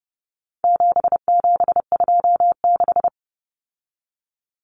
Morse code, heard as a keyed tone, about 30 wpm, 700 Hz. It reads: 7726
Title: Numbers - 4-Digit 30wpm